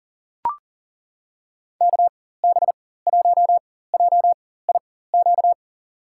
Morse code, heard as a keyed tone, about 40 wpm, spaced as for 19 wpm, 700 Hz. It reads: KB1JIQ
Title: USA Call Signs - 4x Character Spacing - 40wpm